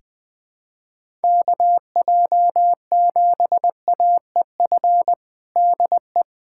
Morse code, heard as a keyed tone, about 19 wpm, 700 Hz. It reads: KJ7AEF DE
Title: USA Callsign DE Callsign 20wpm